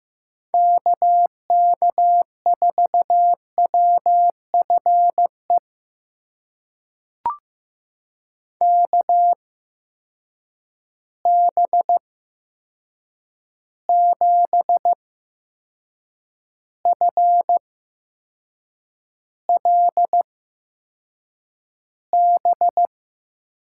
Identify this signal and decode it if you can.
Morse code, heard as a keyed tone, about 15 wpm, spaced as for 6 wpm, 700 Hz. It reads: KK4WFE KB7FLB
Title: USA Call Signs - 6 Characters - 8x Character Spacing - 15wpm